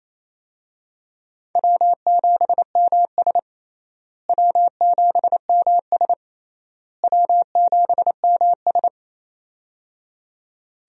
Morse code, heard as a keyed tone, about 28 wpm, 700 Hz. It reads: W7MS W7MS W7MS
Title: USA Call Signs - 4 Characters - Repeated 3x - 28wpm